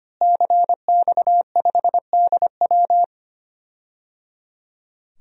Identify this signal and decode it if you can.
Morse code, heard as a keyed tone, about 25 wpm, 700 Hz. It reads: CX5DW
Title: DXCC-100 Call Signs - 5 Characters - Repeated 3x - 25wpm